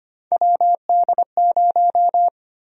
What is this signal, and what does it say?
Morse code, heard as a keyed tone, about 25 wpm, 700 Hz. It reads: WD0
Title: USA Call Sign Prefixes - Repeated 2x - 25wpm